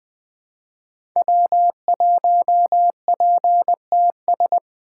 Morse code, heard as a keyed tone, about 20 wpm, 700 Hz. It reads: W1PTS
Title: 2023 SOTA Activator Call Signs - 20wpm